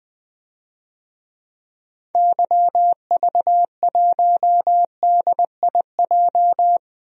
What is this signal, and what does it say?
Morse code, heard as a keyed tone, about 20 wpm, 700 Hz. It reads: YV1DIJ
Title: DXCC-100 Call Signs - 6 Characters - 20wpm